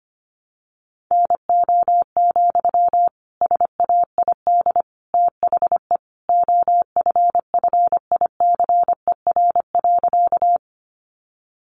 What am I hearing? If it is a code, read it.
Morse code, heard as a keyed tone, about 25 wpm, 700 Hz. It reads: NO, SAID THE OFFICER.
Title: Sentences from Top 1200 Words 25wpm